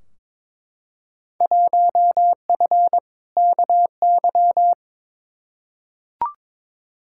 Morse code, heard as a keyed tone, about 21 wpm, 700 Hz. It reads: 1F KY
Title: Field Day Exchanges Encore - 22wpm